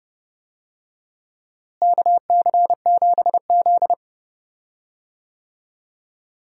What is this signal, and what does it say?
Morse code, heard as a keyed tone, about 30 wpm, 700 Hz. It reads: KC7Z
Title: USA Call Signs - 4 Characters - Repeated 2x - 30wpm